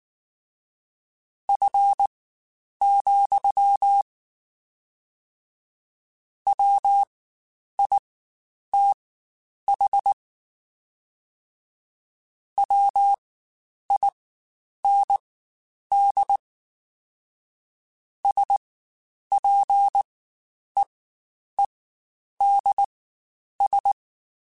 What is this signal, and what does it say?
Morse code, heard as a keyed tone, about 19 wpm, spaced as for 9 wpm, 800 Hz. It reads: F, WITH WIND SPEEDS
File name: Spaced Characters / Spaced Characters-mp3s / Weather 4 10 sp @18